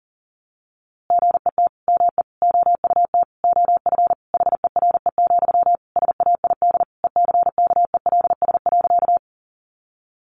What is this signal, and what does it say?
Morse code, heard as a keyed tone, about 40 wpm, 700 Hz. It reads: GET ME OUT OF HERE, SAID ECKELS.
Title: A Sound Of Thunder 40wpm